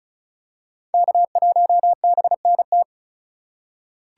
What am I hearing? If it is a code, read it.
Morse code, heard as a keyed tone, about 34 wpm, 700 Hz. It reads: K1BNT
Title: USA Call Signs - 5 Characters - 4x Character Spacing - 35wpm